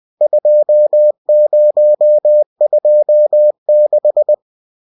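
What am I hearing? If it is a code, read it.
Morse code, heard as a keyed tone, about 20 wpm, 600 Hz. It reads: 2026
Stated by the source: Bulletin sent in CW.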